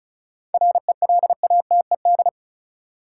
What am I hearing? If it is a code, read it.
Morse code, heard as a keyed tone, about 34 wpm, 700 Hz. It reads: RELATED
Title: Sentences from Top 1400 Words - New Word Drill Alt - 35wpm